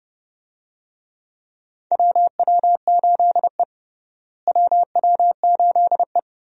Morse code, heard as a keyed tone, about 30 wpm, 700 Hz. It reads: WW8E WW8E
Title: USA Call Signs - 4 Characters - Repeated 3x - 30wpm